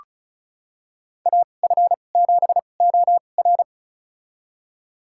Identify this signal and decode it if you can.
Morse code, heard as a keyed tone, about 36 wpm, spaced as for 25 wpm, 700 Hz. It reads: AF7OR
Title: USA Call Signs - 5 Characters - 2x Character Spacing - 35wpm